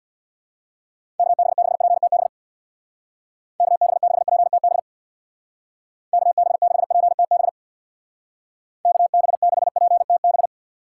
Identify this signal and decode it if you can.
Morse code, heard as a keyed tone, about 65 wpm, 700 Hz. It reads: KB6PTB KB6PTB KB6PTB KB6PTB